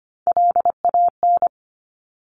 Morse code, heard as a keyed tone, about 25 wpm, 700 Hz. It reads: LAN
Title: Top Three Letter Combinations in Words 25wpm